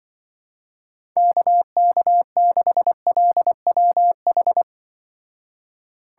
Morse code, heard as a keyed tone, about 25 wpm, 700 Hz. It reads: KK6LWH
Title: Speed Racing - USA 2x3 Call Signs 20wpm